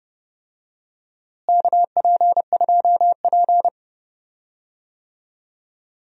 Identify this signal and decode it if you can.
Morse code, heard as a keyed tone, about 30 wpm, 700 Hz. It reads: KP2P
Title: DXCC-100 Call Signs - 4 Characters - Repeated 2x - 30wpm